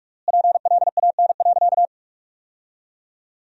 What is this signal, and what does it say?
Morse code, heard as a keyed tone, about 42 wpm, 700 Hz. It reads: PLAN.
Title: 3x Word-Spacing - Sentences from Top 500 Words - 45wpm